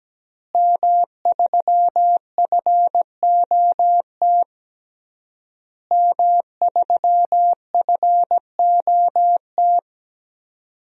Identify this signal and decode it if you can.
Morse code, heard as a keyed tone, about 17 wpm, 700 Hz. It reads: M3FOT M3FOT
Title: DXCC-100 Call Signs - 5 Characters - Repeated 3x - 17wpm